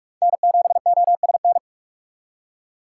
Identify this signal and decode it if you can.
Morse code, heard as a keyed tone, about 47 wpm, 690 Hz. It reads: N7OSN
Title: USA Call Signs - 5 Characters - 6x Character Spacing - 45wpm